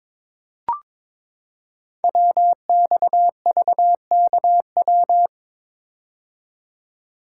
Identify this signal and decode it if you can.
Morse code, heard as a keyed tone, about 22 wpm, 700 Hz. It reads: WXVKW
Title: Code Groups - 5 Letters - 22wpm